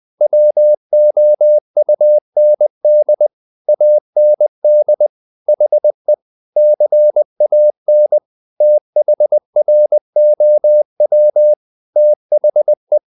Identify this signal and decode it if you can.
Morse code, heard as a keyed tone, about 20 wpm, 600 Hz. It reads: WOUND AND HE CAN THROW THE
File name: the old man and the sea part 2 20wpm